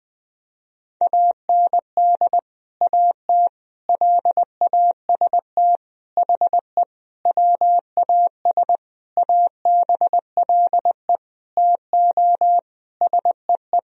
Morse code, heard as a keyed tone, about 21 wpm, 700 Hz. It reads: AND AT LAST HE WAS ABLE TO SEE
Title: Sentences from Top 1000 Words - Review Part 2 - 20wpm